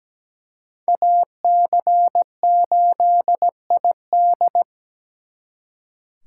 Morse code, heard as a keyed tone, about 20 wpm, 700 Hz. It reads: AC8ID
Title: Speed Racing - USA 2x2 Call Signs 17wpm